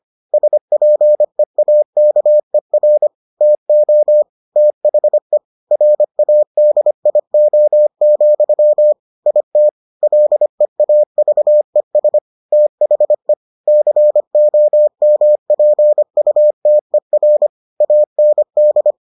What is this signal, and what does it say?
Morse code, heard as a keyed tone, about 25 wpm, 600 Hz. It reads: SPEAKER TO THE RADIO, IT LEAVES THE COMPUTER AND